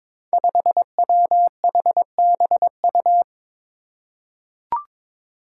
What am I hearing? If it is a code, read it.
Morse code, heard as a keyed tone, about 22 wpm, 700 Hz. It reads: 5WHBU
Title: Five Challenging Letter-Number Combinations 22wpm